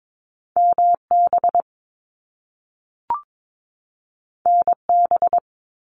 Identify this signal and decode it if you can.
Morse code, heard as a keyed tone, about 22 wpm, 700 Hz. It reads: MB NB
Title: Canadian Provinces and Territories 22wpm